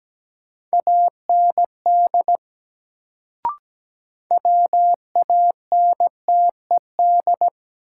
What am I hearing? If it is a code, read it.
Morse code, heard as a keyed tone, about 17 wpm, 700 Hz. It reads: AND WANTED
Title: Sets of 2 Words - New Word Drill - 17wpm